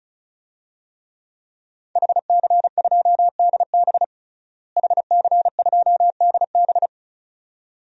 Morse code, heard as a keyed tone, about 34 wpm, 700 Hz. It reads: HC2DB HC2DB
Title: DXCC-100 Call Signs - 5 Characters - Repeated 2x - 35wpm